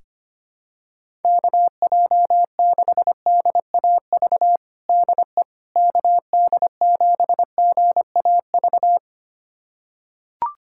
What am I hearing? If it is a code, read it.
Morse code, heard as a keyed tone, about 25 wpm, 700 Hz. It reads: KJ6DAV DE KD7GAV
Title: USA Callsign DE Callsign - 6 Characters - 25wpm